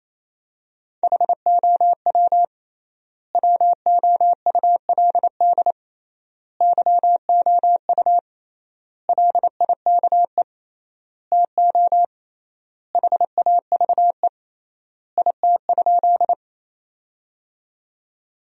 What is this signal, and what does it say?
Morse code, heard as a keyed tone, about 28 wpm, 700 Hz. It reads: HOW WOULD YOU LIKE TO HAVE IT?
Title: 3x Word-Spacing - Sentences from Top 100 Words - 28wpm